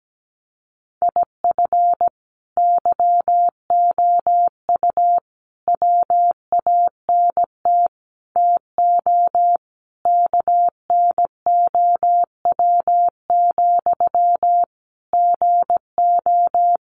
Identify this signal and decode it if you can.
Morse code, heard as a keyed tone, about 17 wpm, 700 Hz. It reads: IF YOU WANT TO KNOW, GO
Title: Sentences from Top 200 Words 17wpm